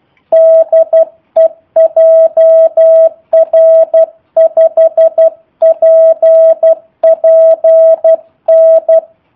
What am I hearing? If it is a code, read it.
Morse code, heard as a keyed tone, about 12 wpm, 650 Hz. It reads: DE JR5PPN
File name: jh8cbh / 2019cw / sp03- w